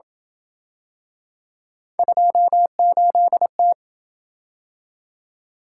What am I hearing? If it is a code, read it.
Morse code, heard as a keyed tone, about 28 wpm, 700 Hz. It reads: 28T